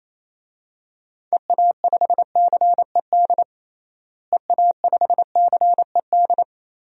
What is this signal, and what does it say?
Morse code, heard as a keyed tone, about 28 wpm, 700 Hz. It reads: EA5CED EA5CED
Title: DXCC-100 Call Signs - 6 Characters - Repeated 3x - 28wpm